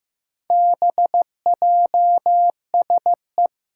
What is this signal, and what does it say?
Morse code, heard as a keyed tone, about 15 wpm, 700 Hz. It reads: BJSE
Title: Four Challenging Letter-Number Combinations 15wpm